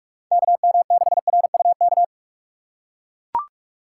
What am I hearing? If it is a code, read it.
Morse code, heard as a keyed tone, about 44 wpm, 700 Hz. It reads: KM6RUK